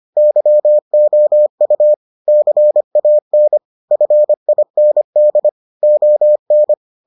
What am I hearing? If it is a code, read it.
Morse code, heard as a keyed tone, about 25 wpm, 600 Hz. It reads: YOU CAN FIND ON